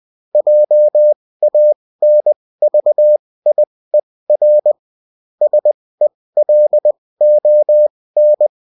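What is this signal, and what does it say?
Morse code, heard as a keyed tone, about 20 wpm, spaced as for 16 wpm, 600 Hz. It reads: JANVIER SELON
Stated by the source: Bulletin in text and CW form.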